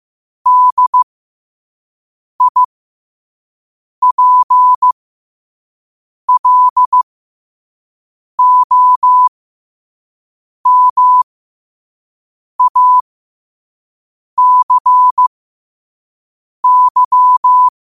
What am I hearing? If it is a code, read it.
Morse code, heard as a keyed tone, about 15 wpm, spaced as for 5 wpm, 1000 Hz. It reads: DIPLOMACY